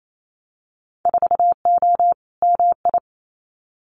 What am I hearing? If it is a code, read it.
Morse code, heard as a keyed tone, about 28 wpm, 700 Hz. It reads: 4O MI
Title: Winter Field Day Exchanges Encore 28wpm